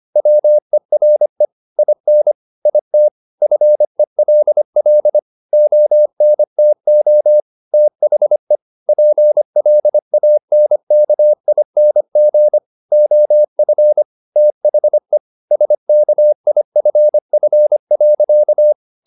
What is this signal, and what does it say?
Morse code, heard as a keyed tone, about 25 wpm, 600 Hz. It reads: WERE IN IT FELL ONTO THE PLANKING OF THE SKIFF.